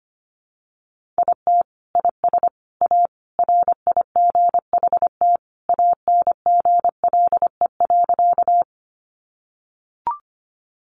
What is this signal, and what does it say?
Morse code, heard as a keyed tone, about 25 wpm, 700 Hz. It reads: IT IS A RIGHT ANGLE.
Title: Sentences from Top 1500 Words 25wpm